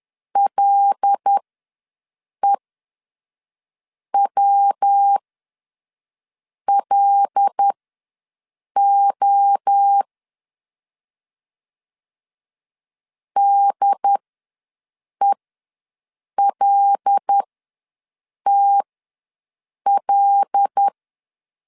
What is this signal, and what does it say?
Morse code, heard as a keyed tone, about 10 wpm, spaced as for 5 wpm, 800 Hz. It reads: LEWLO DELTL
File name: LEZIONI / 5 LBVC / 5 L ws 10wpm 3636